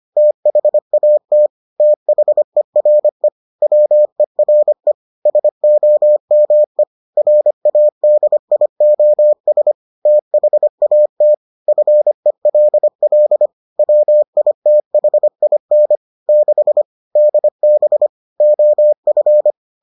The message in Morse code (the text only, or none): THAT THERE WERE SOME RADIOS THAT FELL WITHIN 6 DB OF